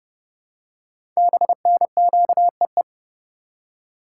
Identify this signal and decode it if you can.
Morse code, heard as a keyed tone, about 28 wpm, 700 Hz. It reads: BNQEE
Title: Code Groups - 5 Letters - 30wpm